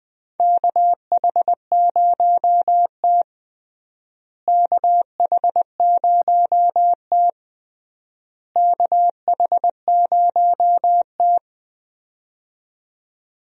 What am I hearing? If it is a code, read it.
Morse code, heard as a keyed tone, about 20 wpm, 700 Hz. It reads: KH0T KH0T KH0T
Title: USA Call Signs - 4 Characters - Repeated 3x - 20wpm